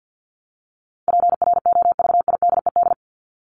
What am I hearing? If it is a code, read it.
Morse code, heard as a keyed tone, about 49 wpm, 700 Hz. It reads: PROVIDED
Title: ED Suffix - Top 100 Words with Base Words 50wpm